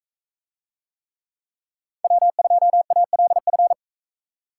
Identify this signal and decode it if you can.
Morse code, heard as a keyed tone, about 41 wpm, 700 Hz. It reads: W2ALF